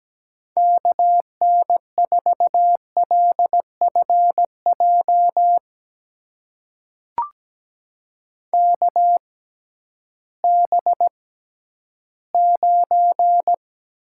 Morse code, heard as a keyed tone, about 17 wpm, spaced as for 11 wpm, 700 Hz. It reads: KN4LFJ KB9
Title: USA Call Signs - 6 Characters - 6x Character Spacing - 17wpm